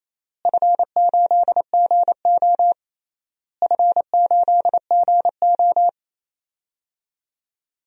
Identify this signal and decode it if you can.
Morse code, heard as a keyed tone, about 28 wpm, 700 Hz. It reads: F8GO F8GO
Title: DXCC-100 Call Signs - 4 Characters - Repeated 3x - 28wpm